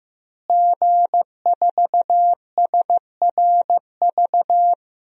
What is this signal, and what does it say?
Morse code, heard as a keyed tone, about 15 wpm, 700 Hz. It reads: G4SRV
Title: DXCC-100 Call Signs - 5 Characters - Repeated 3x - 15wpm